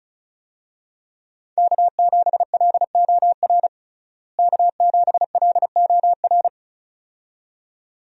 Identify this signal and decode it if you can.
Morse code, heard as a keyed tone, about 34 wpm, 700 Hz. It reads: K7LOR K7LOR
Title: USA Call Signs - 5 Characters - Repeated 2x - 35wpm